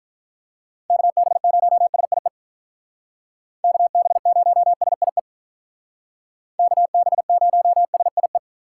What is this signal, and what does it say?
Morse code, heard as a keyed tone, about 44 wpm, 700 Hz. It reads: KB0SIE KB0SIE KB0SIE